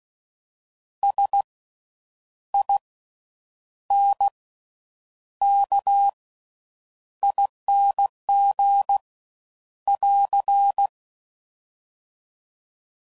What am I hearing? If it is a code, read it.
Morse code, heard as a keyed tone, about 16 wpm, spaced as for 8 wpm, 800 Hz. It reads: SINKING <AR>
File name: Slow With Faster Ings